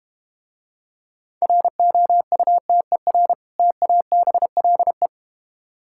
Morse code, heard as a keyed tone, about 32 wpm, 700 Hz. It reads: ROUTER TABLE